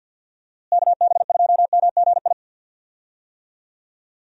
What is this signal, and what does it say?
Morse code, heard as a keyed tone, about 50 wpm, 700 Hz. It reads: KB2MGI
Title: USA Call Signs - 6 Characters - Repeated 2x - 50wpm